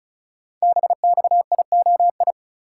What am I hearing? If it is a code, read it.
Morse code, heard as a keyed tone, about 36 wpm, 700 Hz. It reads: BXIOI
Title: Code Groups - 5 Letters - 35wpm